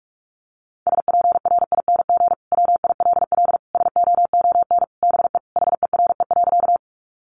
Morse code, heard as a keyed tone, about 44 wpm, 700 Hz. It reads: SPRING WILL SOON BE HERE.
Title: Sentences from Top 1200 Words 45wpm